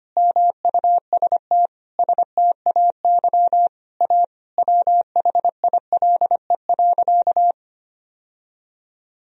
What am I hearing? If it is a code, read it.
Morse code, heard as a keyed tone, about 25 wpm, 700 Hz. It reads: MUST STAY A WHILE.
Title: Sentences from Top 1000 Words - Review Part 1 - 25wpm